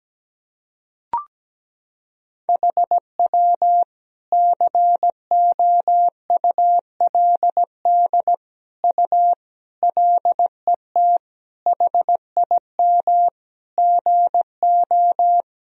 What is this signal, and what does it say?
Morse code, heard as a keyed tone, about 17 wpm, 700 Hz. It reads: HW COULD U LET HIM GO
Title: Sentences from Top 300 Words and Abbreviations 17wpm